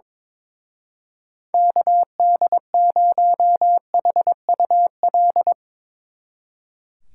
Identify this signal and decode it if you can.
Morse code, heard as a keyed tone, about 24 wpm, 700 Hz. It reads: KD0HUL
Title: Speed Racing - USA 2x3 Call Signs 22wpm